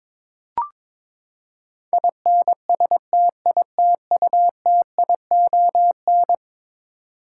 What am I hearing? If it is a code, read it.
Morse code, heard as a keyed tone, about 22 wpm, 700 Hz. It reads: INSTITUTION
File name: practice / 22wpm / Eleven Letter Words - 22wpm